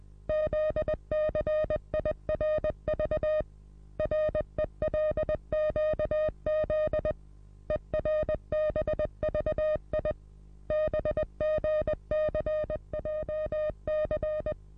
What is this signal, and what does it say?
Morse code, heard as a keyed tone, about 20 wpm, 620 Hz. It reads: ZCIRV RELQZ ERBVI BGCJC
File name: repos / Morsekurs / DJ4UF Morsen27